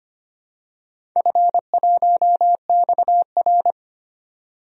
Morse code, heard as a keyed tone, about 25 wpm, 700 Hz. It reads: F1XR
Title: DXCC-100 Call Signs - 4 Characters - 25wpm